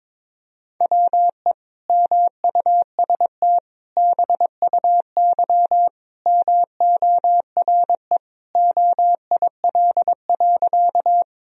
Morse code, heard as a keyed tone, about 22 wpm, 700 Hz. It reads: WE MUST BUY MORE OIL.
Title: Sentences from Top 1000 Words - Review Part 4 - 22wpm